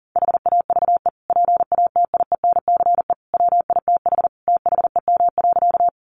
Morse code, heard as a keyed tone, about 40 wpm, 700 Hz. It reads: HAVE PATIENCE WITH THEM.
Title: Sentences from Top 1300 Words 40wpm